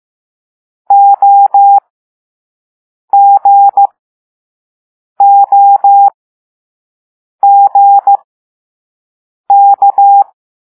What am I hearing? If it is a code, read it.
Morse code, heard as a keyed tone, about 15 wpm, spaced as for 5 wpm, 800 Hz. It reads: OGOGK
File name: Morse / Level4A